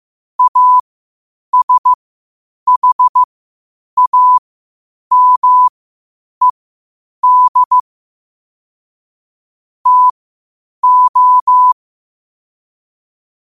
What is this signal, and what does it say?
Morse code, heard as a keyed tone, about 15 wpm, spaced as for 8 wpm, 1000 Hz. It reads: ASHAMED TO
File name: qotdfiles / 8wpm